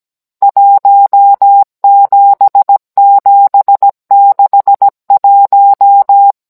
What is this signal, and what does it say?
Morse code, heard as a keyed tone, about 17 wpm, 800 Hz. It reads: 17761
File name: Cours-NR19-16mm